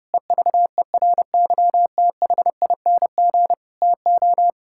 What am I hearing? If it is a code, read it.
Morse code, heard as a keyed tone, about 30 wpm, 700 Hz. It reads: EVERYTHING TO
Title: Sentences from Top 1000 Words - Review Part 6 - 30wpm